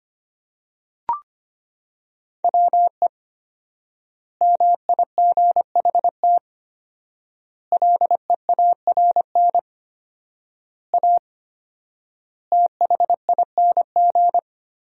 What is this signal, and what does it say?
Morse code, heard as a keyed tone, about 25 wpm, 700 Hz. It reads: WE MIGHT LEARN A THING
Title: 4x Word-Spacing - Sentences from Top 300 Words - 25wpm